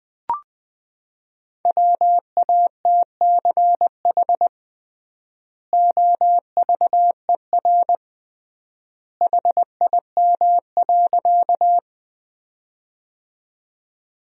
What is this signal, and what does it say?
Morse code, heard as a keyed tone, about 19 wpm, 700 Hz. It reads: WATCH OVER HIM.
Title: 3x Word-Spacing - Sentences from Top 300 Words - 20wpm